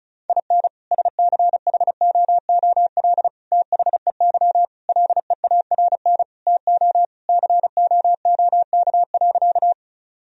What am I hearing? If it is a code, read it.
Morse code, heard as a keyed tone, about 34 wpm, 700 Hz. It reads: IN SCHOOL THEY LEARN TO COOK.
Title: Sentences from Top 1000 Words - Review Part 4 - 35wpm